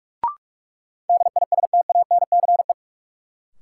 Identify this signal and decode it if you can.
Morse code, heard as a keyed tone, about 46 wpm, 700 Hz. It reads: DISTANCE
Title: Sentences from Top 1200 Words - New Word Drill Alt - 45wpm